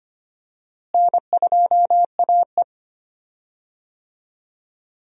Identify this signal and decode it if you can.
Morse code, heard as a keyed tone, about 25 wpm, 700 Hz. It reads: N2AE
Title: USA Call Signs - 4 Characters - Repeated 3x - 25wpm